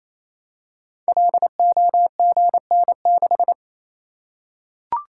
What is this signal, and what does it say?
Morse code, heard as a keyed tone, about 29 wpm, 700 Hz. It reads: LOGN6
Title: Code Groups - 5 Letter:Numbers - 28wpm